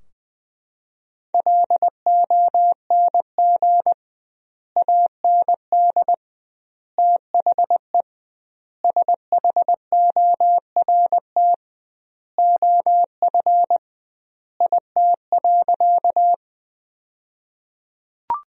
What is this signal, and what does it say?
Morse code, heard as a keyed tone, about 20 wpm, 700 Hz. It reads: LONG AND THE SHORT OF IT.
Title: 2x Word-Spacing - Sentences from Top 400 Words - 20wpm